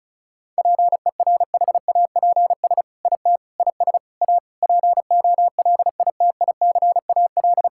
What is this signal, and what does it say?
Morse code, heard as a keyed tone, about 36 wpm, 700 Hz. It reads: PERHAPS IT IS A POLITICAL
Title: Sentences from Top 1000 Words - Review Part 5 - 35wpm